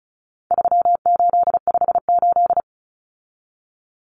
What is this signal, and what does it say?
Morse code, heard as a keyed tone, about 34 wpm, 700 Hz. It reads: 3858
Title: Common Cut Numbers - 4-Digit 35wpm